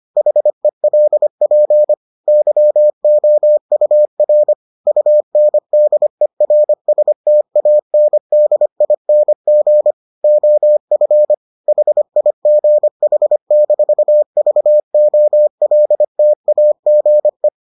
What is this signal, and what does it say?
Morse code, heard as a keyed tone, about 25 wpm, 600 Hz. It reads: HELP YOUR UNDERSTANDING OF HIGH-VOLTAGE